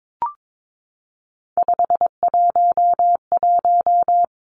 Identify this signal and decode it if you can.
Morse code, heard as a keyed tone, about 22 wpm, 700 Hz. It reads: 511
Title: Three Challenging Letter-Number Combinations 22wpm